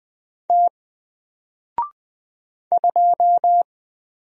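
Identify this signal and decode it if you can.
Morse code, heard as a keyed tone, about 20 wpm, 700 Hz. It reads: T 2
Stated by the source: Single letter and number Morse drill at 20 wpm, from N4LCD.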